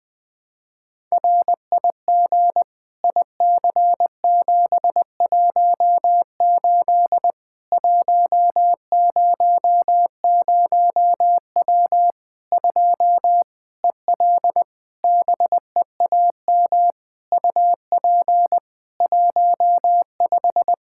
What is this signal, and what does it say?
Morse code, heard as a keyed tone, about 20 wpm, 700 Hz. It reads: RIG IC718 100W 2 EL BEAM UP 15
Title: Practice QSO - Exchange 2 - Info - 20wpm